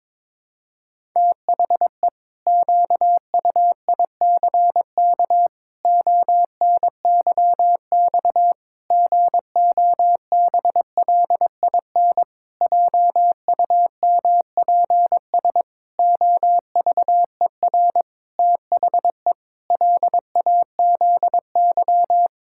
Morse code, heard as a keyed tone, about 22 wpm, 700 Hz. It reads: THE QUICK ONYX GOBLIN JUMPS OVER THE LAZY
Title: Pangram Challenge v2 - 22wpm